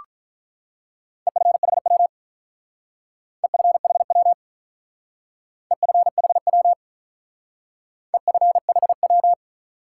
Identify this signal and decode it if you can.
Morse code, heard as a keyed tone, about 45 wpm, 700 Hz. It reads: EFHW EFHW EFHW EFHW